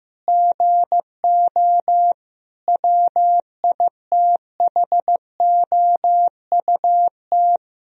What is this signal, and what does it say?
Morse code, heard as a keyed tone, about 16 wpm, 700 Hz. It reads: GO WITHOUT
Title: Sentences from Top 1000 Words - Review Part 3 - 15wpm